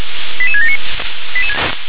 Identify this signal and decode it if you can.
Five-tone selective call (ZVEI), heard as quick runs of five tones, tones of about 70 ms. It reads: 88679